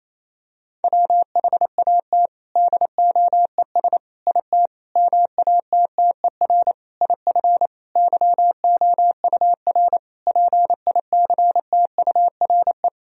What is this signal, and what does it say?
Morse code, heard as a keyed tone, about 28 wpm, 700 Hz. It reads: WHAT DOES IT MATTER IF YOUR PICTURE
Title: Sentences from Top 1000 Words - Review Part 1 - 28wpm